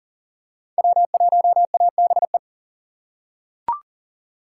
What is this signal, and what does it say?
Morse code, heard as a keyed tone, about 40 wpm, 700 Hz. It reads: W1ABE
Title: USA Call Signs - 5 Characters - 4x Character Spacing - 40wpm